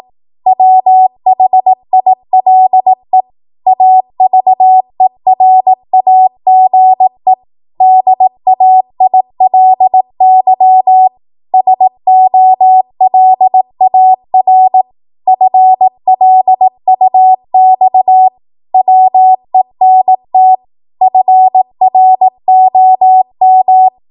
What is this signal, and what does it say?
Morse code, heard as a keyed tone, about 18 wpm, 750 Hz. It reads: WHILE AVERAGE DAILY SOLAR FLUX WENT FROM
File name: Archive / 18 WPM / 18WPM